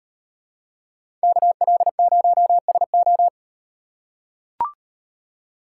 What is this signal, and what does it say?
Morse code, heard as a keyed tone, about 38 wpm, 700 Hz. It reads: KL0SO